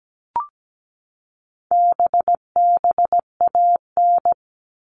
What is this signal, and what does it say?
Morse code, heard as a keyed tone, about 17 wpm, 700 Hz. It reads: BBAN
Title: All Cut Numbers - 4-Digit 17wpm